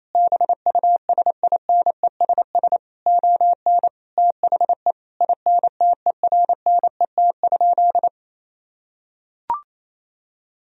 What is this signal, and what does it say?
Morse code, heard as a keyed tone, about 28 wpm, 700 Hz. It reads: BUSINESS ON THE INTERNET?
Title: Sentences from Top 1000 Words - Review Part 4 - 28wpm